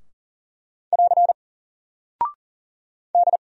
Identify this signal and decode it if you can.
Morse code, heard as a keyed tone, about 40 wpm, 700 Hz. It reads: <AR> D
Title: Single Character (Letter:Number:Punctuation:Prosign) Alt 40wpm